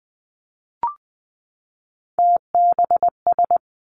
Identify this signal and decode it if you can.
Morse code, heard as a keyed tone, about 20 wpm, 700 Hz. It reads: TBS
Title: Three Challenging Letter-Number Combinations 20wpm